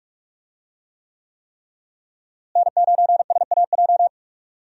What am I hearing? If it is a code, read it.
Morse code, heard as a keyed tone, about 43 wpm, 700 Hz. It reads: N9SAJ
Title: USA Call Signs - 5 Characters - Repeated 3x - 45wpm